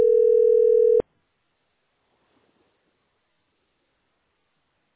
Ringback tone (North America), 440 + 480 Hz.